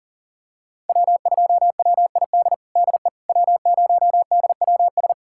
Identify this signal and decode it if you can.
Morse code, heard as a keyed tone, about 40 wpm, 700 Hz. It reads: W2WID DE W0DWS
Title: USA Callsign DE Callsign - 5 Characters - 40wpm